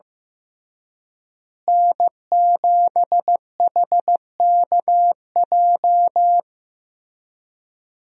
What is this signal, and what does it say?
Morse code, heard as a keyed tone, about 15 wpm, 700 Hz. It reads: N7HKJ
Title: USA Call Signs - 5 Characters - Repeated 3x - 15wpm